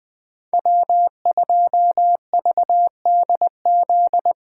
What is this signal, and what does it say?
Morse code, heard as a keyed tone, about 20 wpm, 700 Hz. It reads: W2VDZ
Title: 2023 POTA Hunter Call Signs - 20wpm